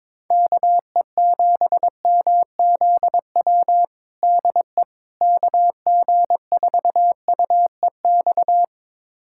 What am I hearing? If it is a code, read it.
Morse code, heard as a keyed tone, about 22 wpm, 700 Hz. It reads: KE7MZW DE KG4UEX
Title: USA Callsign DE Callsign - 6 Characters - 22wpm